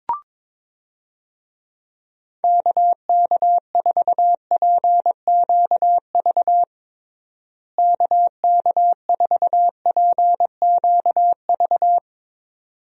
Morse code, heard as a keyed tone, about 22 wpm, 700 Hz. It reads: KK4PQV KK4PQV
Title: USA Call Signs - 6 Characters - Repeated 2x - 22wpm